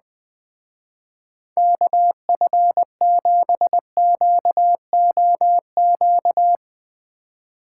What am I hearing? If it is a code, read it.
Morse code, heard as a keyed tone, about 20 wpm, 700 Hz. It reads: KF7QOQ
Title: Speed Racing - USA 2x3 Call Signs 20wpm